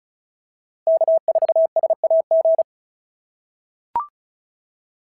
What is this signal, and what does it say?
Morse code, heard as a keyed tone, about 36 wpm, 690 Hz. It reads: K4SAG
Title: SOTA QSO Practice 35wpm